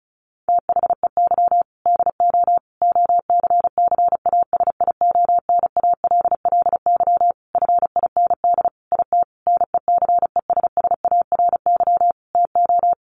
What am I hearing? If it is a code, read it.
Morse code, heard as a keyed tone, about 34 wpm, 700 Hz. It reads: THEY DO OCCASIONALLY FIND IT NECESSARY TO
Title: Sentences from Top 1300 Words 35wpm